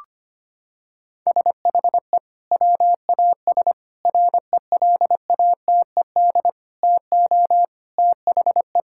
Morse code, heard as a keyed tone, about 25 wpm, 700 Hz. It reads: SHE WAS RELATED TO THE
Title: Sentences from Top 1000 Words - Review Part 5 - 25wpm